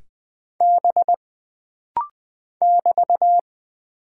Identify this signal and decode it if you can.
Morse code, heard as a keyed tone, about 20 wpm, 700 Hz. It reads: B <BT>
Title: Single Character (Letter:Number:Punctuation:Prosign) - Rapid-Fire - 20wpm